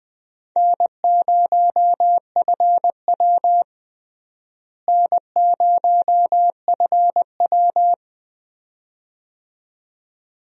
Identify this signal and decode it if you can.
Morse code, heard as a keyed tone, about 21 wpm, 700 Hz. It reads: N0FW N0FW
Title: USA Call Signs - 4 Characters - Repeated 2x - 20wpm